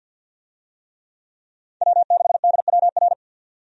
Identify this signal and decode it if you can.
Morse code, heard as a keyed tone, about 45 wpm, 700 Hz. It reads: W6DWR
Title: USA Call Signs - 5 Characters - 50wpm